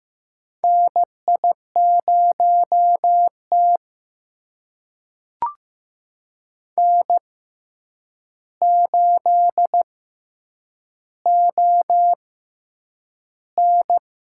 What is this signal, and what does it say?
Morse code, heard as a keyed tone, about 15 wpm, spaced as for 8 wpm, 700 Hz. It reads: NI0T N8ON
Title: USA Call Signs - 4 Characters - 6x Character Spacing - 15wpm